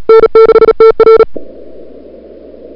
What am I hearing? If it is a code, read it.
Morse code, heard as a keyed tone, about 33 wpm, 440 Hz. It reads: N6TR
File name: CWSprint / February 2007 CW Sprint / 01-2007 CW Sprint Hour1